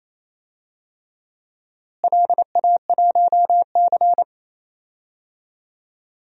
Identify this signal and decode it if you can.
Morse code, heard as a keyed tone, about 28 wpm, 700 Hz. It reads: LA1C
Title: DXCC-100 Call Signs - 4 Characters - Repeated 3x - 28wpm